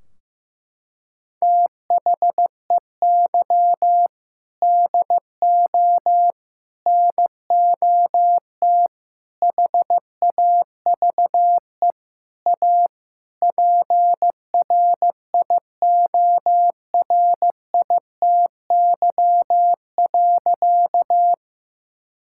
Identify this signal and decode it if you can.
Morse code, heard as a keyed tone, about 15 wpm, 700 Hz. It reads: THEY DO NOT HAVE A PRIORITY.
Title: Sentences from Top 1600 Words 15wpm